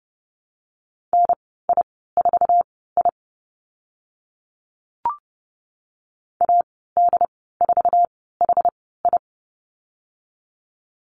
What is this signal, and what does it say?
Morse code, heard as a keyed tone, about 30 wpm, spaced as for 17 wpm, 700 Hz. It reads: NI4I AD4HI
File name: Lesson 14g - Number 2 - Call Signs 30wpm